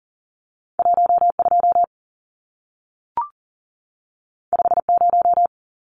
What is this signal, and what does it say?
Morse code, heard as a keyed tone, about 40 wpm, 700 Hz. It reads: 12 50
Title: Numbers - 2-Digit 40wpm